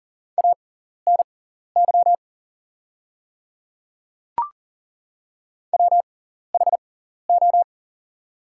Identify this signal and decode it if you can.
Morse code, heard as a keyed tone, about 40 wpm, spaced as for 14 wpm, 690 Hz. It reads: ANY WHO
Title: 6x Character Spacing - Top 200 Words - 40wpm